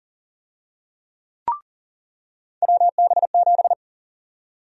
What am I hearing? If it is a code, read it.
Morse code, heard as a keyed tone, about 40 wpm, 700 Hz. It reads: WB7
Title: USA Call Sign Prefixes - 40wpm